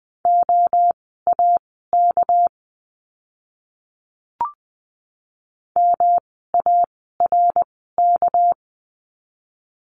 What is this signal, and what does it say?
Morse code, heard as a keyed tone, about 20 wpm, spaced as for 14 wpm, 700 Hz. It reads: OAK MARK
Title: 199 Lesson 32f - Letter k - Words 20wpm